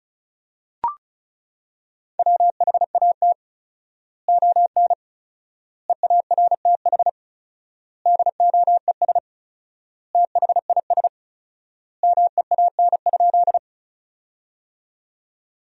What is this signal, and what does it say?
Morse code, heard as a keyed tone, about 34 wpm, 700 Hz. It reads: WHAT ON EARTH DOES THIS MEAN?
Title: 4x Word-Spacing - Sentences from Top 200 Words - 35wpm